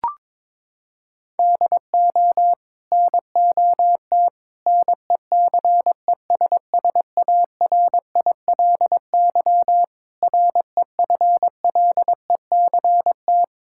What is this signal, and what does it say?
Morse code, heard as a keyed tone, about 21 wpm, 700 Hz. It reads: DO NOT NECESSARILY REFLECT
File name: practice / 22wpm / Sets of 8 Words - 22wpm